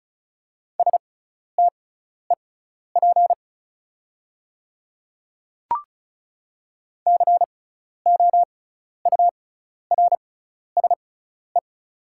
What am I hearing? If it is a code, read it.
Morse code, heard as a keyed tone, about 36 wpm, spaced as for 12 wpm, 700 Hz. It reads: STEP COURSE
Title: 6x Character Spacing - Top 500 Words - 35wpm